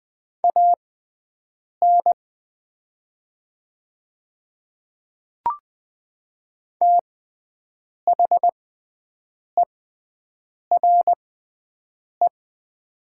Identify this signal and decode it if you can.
Morse code, heard as a keyed tone, about 20 wpm, spaced as for 7 wpm, 700 Hz. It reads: AN THERE
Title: 6x Character Spacing - Top 300 Words - 20wpm